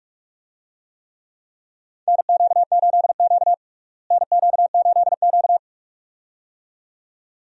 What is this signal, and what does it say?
Morse code, heard as a keyed tone, about 46 wpm, 700 Hz. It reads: NQ8Q NQ8Q
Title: USA Call Signs - 4 Characters - Repeated 2x - 45wpm